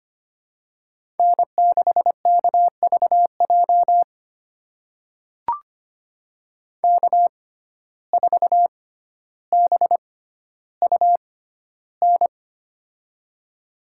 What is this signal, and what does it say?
Morse code, heard as a keyed tone, about 25 wpm, spaced as for 13 wpm, 700 Hz. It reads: N6KVJ K4BUN
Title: USA Call Signs - 5 Characters - 6x Character Spacing - 25wpm